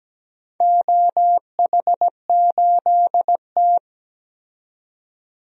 Morse code, heard as a keyed tone, about 17 wpm, 700 Hz. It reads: OH8T
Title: DXCC-100 Call Signs - 4 Characters - 17wpm